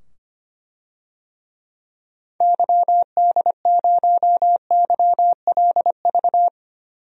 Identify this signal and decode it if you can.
Morse code, heard as a keyed tone, about 25 wpm, 700 Hz. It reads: YD0YLV
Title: DXCC-100 Call Signs - 6 Characters - 25wpm